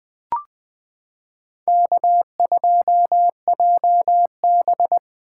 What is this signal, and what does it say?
Morse code, heard as a keyed tone, about 20 wpm, 700 Hz. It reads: K2JB
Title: 2023 SOTA Activator Call Signs - 20wpm